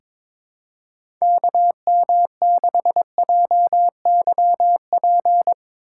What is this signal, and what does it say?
Morse code, heard as a keyed tone, about 23 wpm, 700 Hz. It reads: KM6JYP
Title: USA Call Signs - 8x Character Spacing - 22wpm